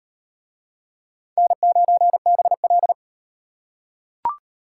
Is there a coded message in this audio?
Yes: Morse code from 1.4 to 2.9 s, heard as a keyed tone, about 33 wpm, 700 Hz. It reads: N9BL